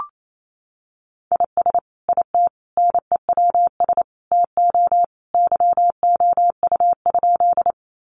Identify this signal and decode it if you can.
Morse code, heard as a keyed tone, about 28 wpm, 700 Hz. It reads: IS IT NEWS TO YOU?
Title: Sentences from Top 800 Words 28wpm